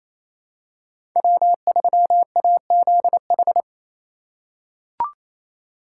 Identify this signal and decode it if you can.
Morse code, heard as a keyed tone, about 28 wpm, 700 Hz. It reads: W3AZH
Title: USA Call Signs - 5 Characters - 4x Character Spacing - 28wpm